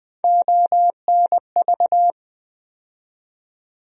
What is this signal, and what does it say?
Morse code, heard as a keyed tone, about 20 wpm, 700 Hz. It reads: ONV
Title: USA Call Sign Postfixes - 3 Letters 20wpm